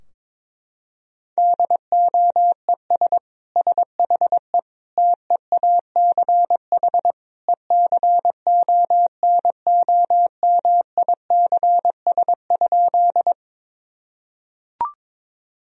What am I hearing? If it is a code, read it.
Morse code, heard as a keyed tone, about 21 wpm, 700 Hz. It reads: DOES SHE TEACH ECONOMICS?
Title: Sentences from Top 1000 Words - Review Part 3 - 22wpm